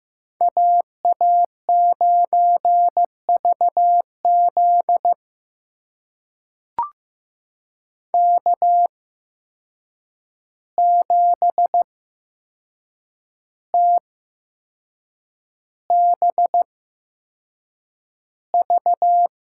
Morse code, heard as a keyed tone, about 15 wpm, spaced as for 6 wpm, 700 Hz. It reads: AA9VZ K7TBV
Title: USA Call Signs - 5 Characters - 8x Character Spacing - 15wpm